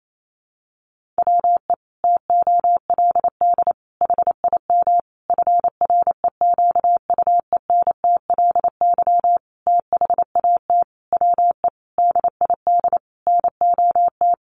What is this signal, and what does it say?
Morse code, heard as a keyed tone, about 28 wpm, 700 Hz. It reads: WE TOLD HIM FREQUENTLY THAT WE DID NOT
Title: Sentences from Top 1300 Words 28wpm